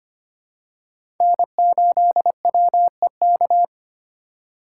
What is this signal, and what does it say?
Morse code, heard as a keyed tone, about 25 wpm, 700 Hz. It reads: N8WEK
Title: USA Call Signs - 3x Character Spacing - 25wpm